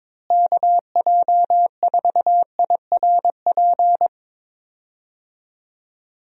Morse code, heard as a keyed tone, about 22 wpm, 700 Hz. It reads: KJ4IRP
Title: USA Call Signs - 6 Characters - Repeated 3x - 22wpm